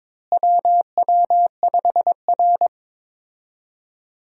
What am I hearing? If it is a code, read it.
Morse code, heard as a keyed tone, about 22 wpm, 700 Hz. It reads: WW5R
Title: USA Call Signs - 4 Characters - Repeated 2x - 22wpm